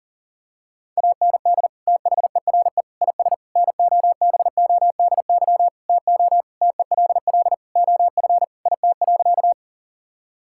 Morse code, heard as a keyed tone, about 40 wpm, 700 Hz. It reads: AND THERE IS NOBODY TO TELL OF IT.
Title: Sentences from Top 1600 Words 40wpm